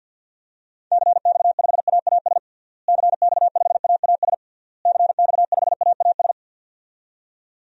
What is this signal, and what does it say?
Morse code, heard as a keyed tone, about 47 wpm, 700 Hz. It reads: CX5AAS CX5AAS CX5AAS
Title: DXCC-100 Call Signs - 6 Characters - Repeated 3x - 50wpm